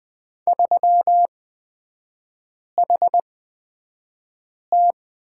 Morse code, heard as a keyed tone, about 20 wpm, spaced as for 5 wpm, 690 Hz. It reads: 3HT
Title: 229 Lesson 37e - Letter x - Three Characters 20wpm